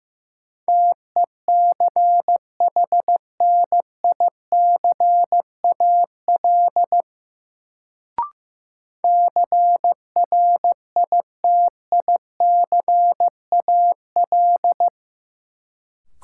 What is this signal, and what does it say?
Morse code, heard as a keyed tone, about 15 wpm, 700 Hz. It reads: TECHNICAL CRITICAL
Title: Sentences from Top 1100 Words - New Word Drill - 15wpm